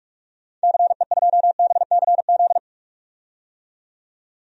Morse code, heard as a keyed tone, about 44 wpm, 700 Hz. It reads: CE2BCZ
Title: DXCC-100 Call Signs - 6 Characters - 45wpm